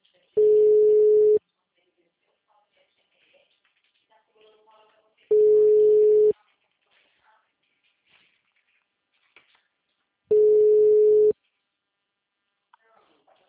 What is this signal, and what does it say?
Ringback tone (Europe, 425 Hz), with speech.
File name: out- -2001- - - .15423